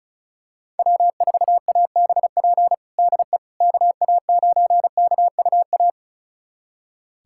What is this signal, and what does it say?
Morse code, heard as a keyed tone, about 36 wpm, 700 Hz. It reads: W4ABP DE KA9KUA
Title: USA Callsign DE Callsign 35wpm